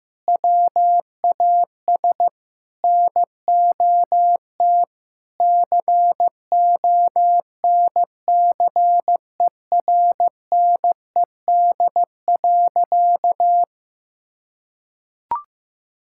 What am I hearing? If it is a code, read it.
Morse code, heard as a keyed tone, about 15 wpm, 700 Hz. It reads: WAS NOT CONCERNED.
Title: Sentences from Top 1000 Words - Review Part 6 - 15wpm